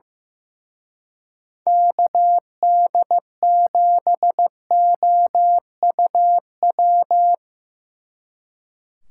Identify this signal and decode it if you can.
Morse code, heard as a keyed tone, about 17 wpm, 700 Hz. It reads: KD7OUW
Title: Speed Racing - USA 2x3 Call Signs 15wpm